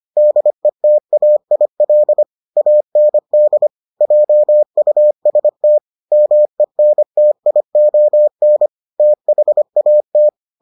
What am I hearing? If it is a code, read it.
Morse code, heard as a keyed tone, about 25 wpm, 600 Hz. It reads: DETAIL AND JUST MENTION THAT